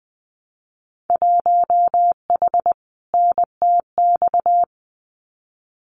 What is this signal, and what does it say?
Morse code, heard as a keyed tone, about 20 wpm, 700 Hz. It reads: 1H NTX
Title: Winter Field Day Exchanges 20wpm